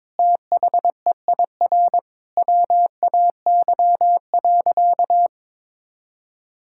Morse code, heard as a keyed tone, about 22 wpm, 700 Hz. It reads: THEIR WAY.
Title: Sentences from Top 1000 Words - Review Part 5 - 22wpm